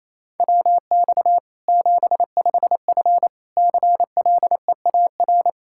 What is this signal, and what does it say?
Morse code, heard as a keyed tone, about 28 wpm, 700 Hz. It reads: WX 75F CLEAR
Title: Practice QSO - Exchange 2 - Info - 28wpm